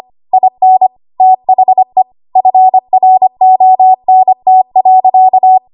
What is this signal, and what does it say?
Morse code, heard as a keyed tone, about 25 wpm, 750 Hz. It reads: IN THE FRONT.
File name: Archive / 25 WPM / 25WPM